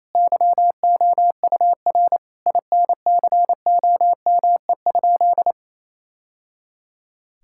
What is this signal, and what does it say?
Morse code, heard as a keyed tone, about 28 wpm, 700 Hz. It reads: YOUR INCOME?
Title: Sentences from Top 1000 Words - Review Part 5 - 28wpm